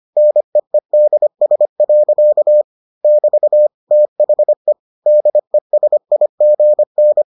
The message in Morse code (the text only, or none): NEEDS. <BT> THE DESIGN